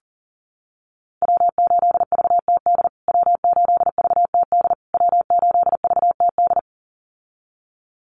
Morse code, heard as a keyed tone, about 40 wpm, 700 Hz. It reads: W8VTD W8VTD W8VTD
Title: USA Call Signs Repeated 3x 40wpm